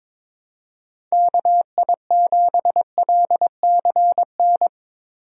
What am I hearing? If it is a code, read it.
Morse code, heard as a keyed tone, about 22 wpm, 700 Hz. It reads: KI7LCN
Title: USA Call Signs - 6 Characters - Repeated 2x - 22wpm